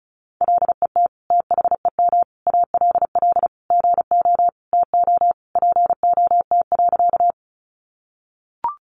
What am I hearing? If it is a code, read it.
Morse code, heard as a keyed tone, about 34 wpm, 700 Hz. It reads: LET THEM ALL GO TO POT.
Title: Sentences from Top 1000 Words 35wpm